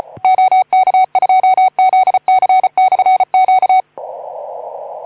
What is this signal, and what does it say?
Morse code, heard as a keyed tone, about 36 wpm, 750 Hz. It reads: OK2ZC/Q
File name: Archiv / CUC181 / OK2ZC CUC 181